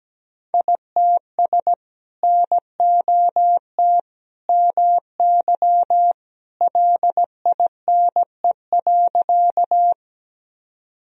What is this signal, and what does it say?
Morse code, heard as a keyed tone, about 17 wpm, 700 Hz. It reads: ITS NOT MY LINE.
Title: Sentences from Top 1000 Words - Review Part 1 - 17wpm